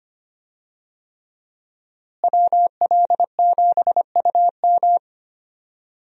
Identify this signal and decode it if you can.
Morse code, heard as a keyed tone, about 25 wpm, 700 Hz. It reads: WL7UM
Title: USA Call Signs - 5 Characters - Repeated 3x - 25wpm